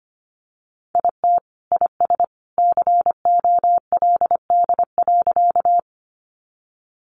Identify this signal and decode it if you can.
Morse code, heard as a keyed tone, about 25 wpm, 700 Hz. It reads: IT IS COLD.
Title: Sentences from Top 400 Words 25wpm